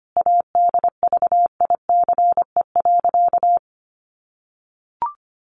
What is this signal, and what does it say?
Morse code, heard as a keyed tone, about 24 wpm, 700 Hz. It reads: ADVICE.
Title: Sentences from Top 1100 Words 25wpm